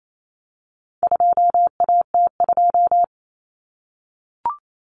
Morse code, heard as a keyed tone, about 28 wpm, 700 Hz. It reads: 2AT2
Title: Common Cut Numbers - 4-Digit 28wpm